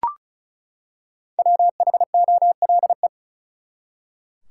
Morse code, heard as a keyed tone, about 34 wpm, 700 Hz. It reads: WHOLE
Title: Five Letter Words - 35wpm